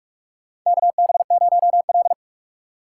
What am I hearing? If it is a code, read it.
Morse code, heard as a keyed tone, about 46 wpm, 700 Hz. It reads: KB0L
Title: USA Call Signs - 4 Characters - 6x Character Spacing - 45wpm